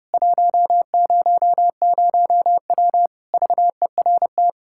Morse code, heard as a keyed tone, about 30 wpm, 700 Hz. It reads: 100W VERT
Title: K1USN SST QSO Practice 30wpm